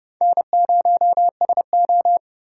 Morse code, heard as a keyed tone, about 29 wpm, 690 Hz. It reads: N0SO